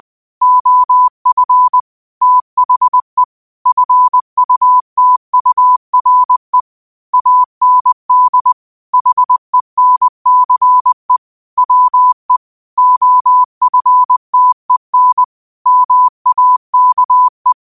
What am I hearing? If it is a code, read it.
Morse code, heard as a keyed tone, about 20 wpm, 1000 Hz. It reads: OF THE FUTURE AND HENCE WE OFTEN MAKE
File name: qotdfiles / 20wpm